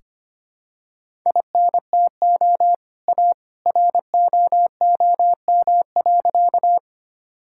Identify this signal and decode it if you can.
Morse code, heard as a keyed tone, about 25 wpm, 700 Hz. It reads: INTO A ROOM.
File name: practice / 25wpm / Sentences with 2-Word Phrases - 25wpm